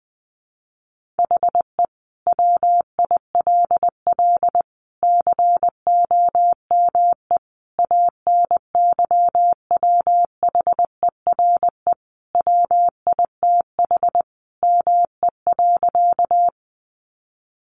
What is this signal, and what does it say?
Morse code, heard as a keyed tone, about 20 wpm, 700 Hz. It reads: HE WILL COME ANYWHERE WITH ME.
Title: Sentences from Top 1400 Words 20wpm